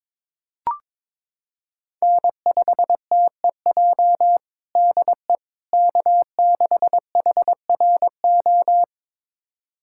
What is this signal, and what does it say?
Morse code, heard as a keyed tone, about 22 wpm, 700 Hz. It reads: N5TEJ DE K6HRO
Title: USA Callsign DE Callsign 22wpm